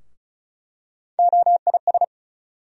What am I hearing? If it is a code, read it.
Morse code, heard as a keyed tone, about 34 wpm, 700 Hz. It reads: OIS
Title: USA Call Sign Postfixes - 3 Letters 35wpm